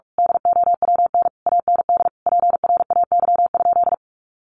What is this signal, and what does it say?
Morse code, heard as a keyed tone, about 44 wpm, 700 Hz. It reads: DOWN AND PRAY?
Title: A Sound Of Thunder 45wpm